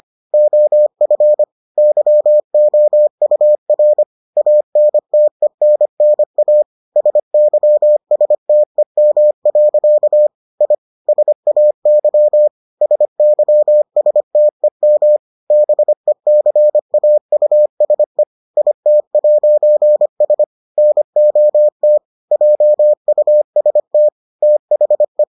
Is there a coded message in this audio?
Yes: Morse code from 0.3 to 25.2 s, heard as a keyed tone, about 25 wpm, 600 Hz. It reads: OF YOUR ANTENNA SYSTEM. I SAY SYSTEM BECAUSE IT'S NOT JUST THE